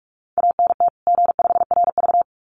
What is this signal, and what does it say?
Morse code, heard as a keyed tone, about 44 wpm, 700 Hz. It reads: ANT G5RV
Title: Common QSO Phrases 45wpm